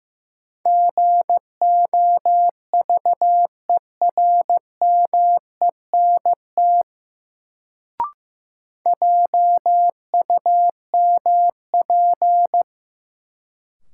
Morse code, heard as a keyed tone, about 16 wpm, 700 Hz. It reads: GOVERMENT JUMP
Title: Sentences from Top 1300 Words - New Word Drill - 15wpm